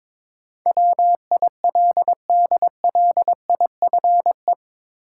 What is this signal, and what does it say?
Morse code, heard as a keyed tone, about 21 wpm, 700 Hz. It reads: WILDLIFE
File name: practice / 22wpm / Eight Letter Words - 22wpm